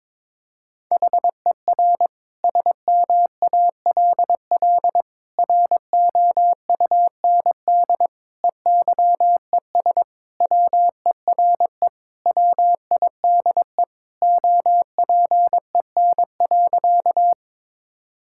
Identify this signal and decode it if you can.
Morse code, heard as a keyed tone, about 22 wpm, 700 Hz. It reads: HER SMALL ROUND EYES WERE WIDE OPEN.
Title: Sentences from Top 1000 Words - Review Part 6 - 22wpm